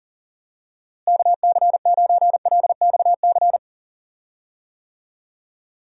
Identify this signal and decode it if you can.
Morse code, heard as a keyed tone, about 37 wpm, 700 Hz. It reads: KC9LXC
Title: USA Call Signs - 6 Characters - Repeated 2x - 40wpm